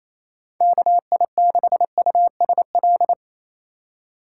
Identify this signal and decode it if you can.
Morse code, heard as a keyed tone, about 34 wpm, 700 Hz. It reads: KI6USL
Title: Speed Racing - USA 2x3 Call Signs 28wpm